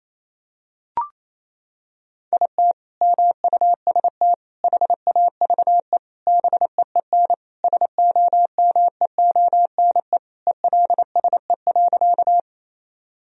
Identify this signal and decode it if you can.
Morse code, heard as a keyed tone, about 28 wpm, 700 Hz. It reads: IT MUST HAVE BEEN SOMEONE ELSE.
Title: Sentences from Top 1000 Words - Review Part 4 - 28wpm